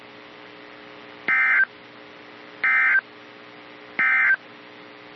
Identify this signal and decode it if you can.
Emergency Alert System (SAME) tones, heard as harsh buzzing bursts. It is heard from 1.2 to 4.4 s.
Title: NOAA Weather Radio - KEC55